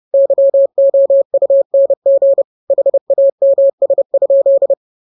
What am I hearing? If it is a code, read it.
Morse code, heard as a keyed tone, about 30 wpm, 550 Hz. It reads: YOUNG HAMS?
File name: jokes / joke 13 30 0000